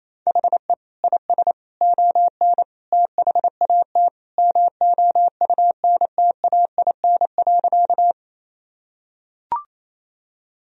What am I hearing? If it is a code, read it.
Morse code, heard as a keyed tone, about 28 wpm, 700 Hz. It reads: HE IS ON THAT MOUNTAIN.
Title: Sentences from Top 1000 Words - Review Part 1 - 28wpm